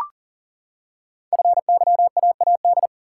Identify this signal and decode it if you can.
Morse code, heard as a keyed tone, about 39 wpm, 700 Hz. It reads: FYAAD
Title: Code Groups - 5 Letters - 40wpm